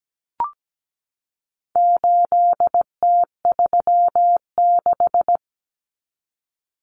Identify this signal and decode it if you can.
Morse code, heard as a keyed tone, about 17 wpm, 700 Hz. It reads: 8T36
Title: Common Cut Numbers - 4-Digit 17wpm